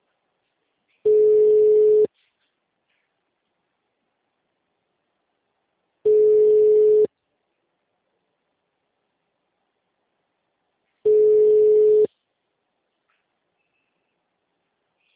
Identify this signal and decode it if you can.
Ringback tone (Europe, 425 Hz).